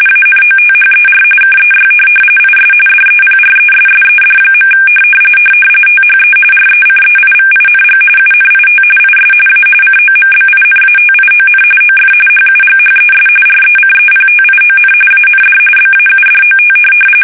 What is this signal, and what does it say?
RTTY, about 45.45 baud. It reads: ANOTHER KEY IS DEPRESSED. THE RECEIVING UNIT IS A COMBINATION OF HUGHES RECEPTION AND BAUDOT